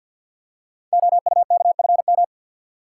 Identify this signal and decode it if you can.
Morse code, heard as a keyed tone, about 51 wpm, 700 Hz. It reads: OUKFM
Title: Code Groups - 5 Letters - 50wpm